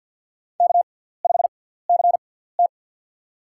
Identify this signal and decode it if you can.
Morse code, heard as a keyed tone, about 51 wpm, spaced as for 17 wpm, 700 Hz. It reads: K5CT
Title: USA Call Signs - 4 Characters - 6x Character Spacing - 50wpm